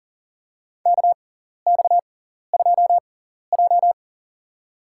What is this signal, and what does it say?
Morse code, heard as a keyed tone, about 40 wpm, spaced as for 14 wpm, 700 Hz. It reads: KX2J
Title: USA Call Signs - 4 Characters - 6x Character Spacing - 40wpm